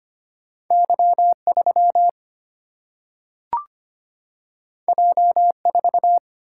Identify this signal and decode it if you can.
Morse code, heard as a keyed tone, about 25 wpm, 700 Hz. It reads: Y3 J4
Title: Single-Letter Single-Number 25wpm